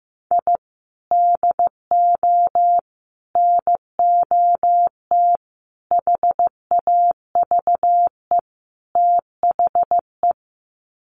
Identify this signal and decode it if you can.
Morse code, heard as a keyed tone, about 15 wpm, 700 Hz. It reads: I DO NOT HAVE THE
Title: Sets of 5 Words 15wpm